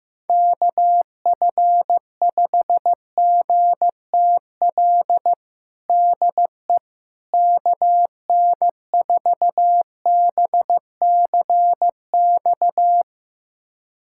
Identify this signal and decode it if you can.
Morse code, heard as a keyed tone, about 15 wpm, 700 Hz. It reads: KF5GTL DE KN4BCX
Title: USA Callsign DE Callsign 15wpm